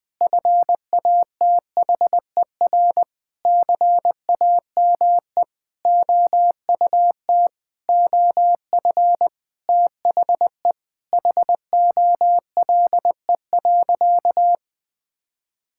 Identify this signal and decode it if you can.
Morse code, heard as a keyed tone, about 20 wpm, 700 Hz. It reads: FATHER CAME OUT OF THE HOLE.
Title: Sentences from Top 1600 Words 20wpm